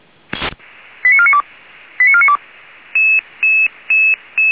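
Five-tone selective call (ZVEI), heard as quick runs of five tones, tones of about 70 ms. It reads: 89492 89492
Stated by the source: ZVEI five-tone selective call, code 89200, received by SDR, from Feuerwehr Orlamünde.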